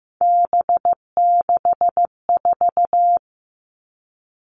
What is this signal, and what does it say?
Morse code, heard as a keyed tone, about 15 wpm, 700 Hz. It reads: B64
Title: All Cut Numbers - 3-Digit 15wpm